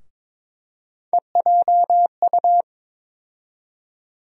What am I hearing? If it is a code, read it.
Morse code, heard as a keyed tone, about 22 wpm, 700 Hz. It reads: EJU
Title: USA Call Sign Postfixes - 3 Letters 22wpm